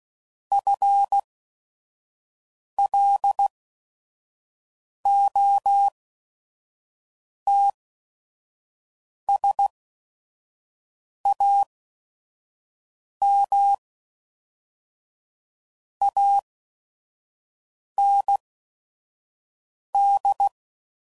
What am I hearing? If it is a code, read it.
Morse code, heard as a keyed tone, about 16 wpm, spaced as for 5 wpm, 800 Hz. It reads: FLOTSAM AND